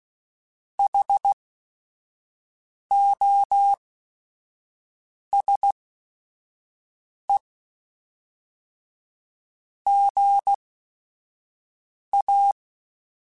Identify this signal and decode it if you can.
Morse code, heard as a keyed tone, about 16 wpm, spaced as for 5 wpm, 800 Hz. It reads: HOSE GA
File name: Technical Subjects / Technical Subjects-mp3s / AND Gate @5